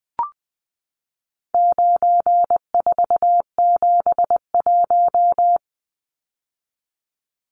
Morse code, heard as a keyed tone, about 20 wpm, 700 Hz. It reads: 9471
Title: Numbers - 4-Digit 20wpm